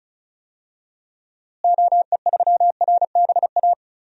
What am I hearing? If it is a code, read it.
Morse code, heard as a keyed tone, about 36 wpm, 700 Hz. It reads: OE3RBA
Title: DXCC-100 Call Signs - 6 Characters - Repeated 3x - 35wpm